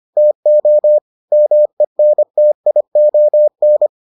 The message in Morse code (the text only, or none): TO MENTION